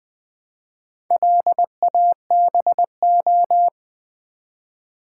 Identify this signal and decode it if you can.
Morse code, heard as a keyed tone, about 20 wpm, 700 Hz. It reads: LABO
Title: Build A Word - 4 Letters At A Time - 10-Letter Words - 20wpm